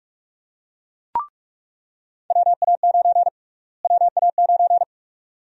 Morse code, heard as a keyed tone, about 46 wpm, 700 Hz. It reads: WA9 WA9
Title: USA Call Sign Prefixes - Repeated 2x - 45wpm